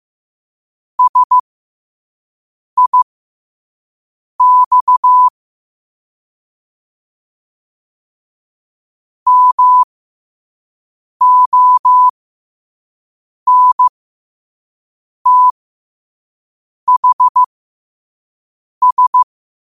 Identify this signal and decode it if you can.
Morse code, heard as a keyed tone, about 15 wpm, spaced as for 5 wpm, 1000 Hz. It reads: SIX MONTHS
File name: qotdfiles / 5wpm